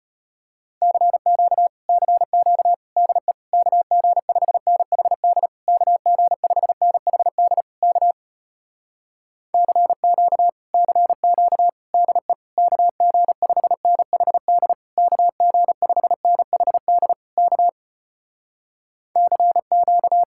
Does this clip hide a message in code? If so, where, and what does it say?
Yes: Morse code from 0.8 to 20.2 s, heard as a keyed tone, about 33 wpm, 700 Hz. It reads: CQ CQ DE KG5NHD KG5NHD K CQ CQ DE KG5NHD KG5NHD K CQ